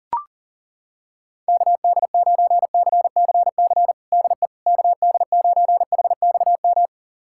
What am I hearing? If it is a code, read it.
Morse code, heard as a keyed tone, about 37 wpm, 690 Hz. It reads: KD9CCC DE KD9HXM
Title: USA Callsign DE Callsign - 6 Characters - 40wpm